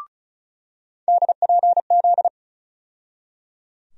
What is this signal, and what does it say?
Morse code, heard as a keyed tone, about 34 wpm, 700 Hz. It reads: DPZ
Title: USA Call Sign Postfixes - 3 Letters 35wpm